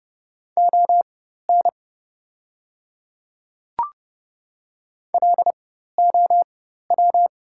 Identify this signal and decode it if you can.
Morse code, heard as a keyed tone, about 30 wpm, spaced as for 14 wpm, 700 Hz. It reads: ON LOW
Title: 4x Character Spacing - Top 300 Words - 30wpm